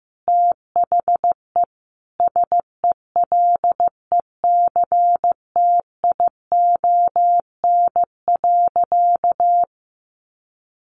Morse code, heard as a keyed tone, about 15 wpm, 700 Hz. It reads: THE SELECTION.
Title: Sentences from Top 1400 Words 15wpm